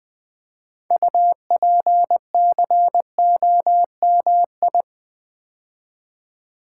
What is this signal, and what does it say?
Morse code, heard as a keyed tone, about 20 wpm, 700 Hz. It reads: UPCOMI
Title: Build A Word - 3 Letters At A Time - 8-Letter Words - 20wpm